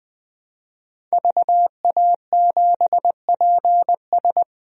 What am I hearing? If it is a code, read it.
Morse code, heard as a keyed tone, about 20 wpm, 700 Hz. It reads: VA7PS
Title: DXCC-100 Call Signs - 5 Characters - 20wpm